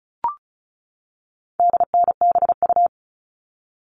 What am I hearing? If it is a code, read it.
Morse code, heard as a keyed tone, about 34 wpm, 700 Hz. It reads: DNBU
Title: All Cut Numbers - 4-Digit 35wpm